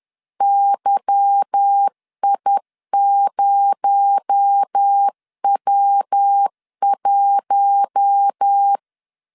Morse code, heard as a keyed tone, about 10 wpm, 800 Hz. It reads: YI0W1
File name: LEZIONI / 8 YJ45 / 8 Y 10wpm 2615